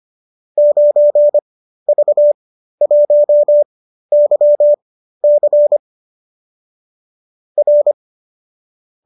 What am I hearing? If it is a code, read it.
Morse code, heard as a keyed tone, about 25 wpm, spaced as for 13 wpm, 570 Hz. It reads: 9V1YC R
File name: qso207 13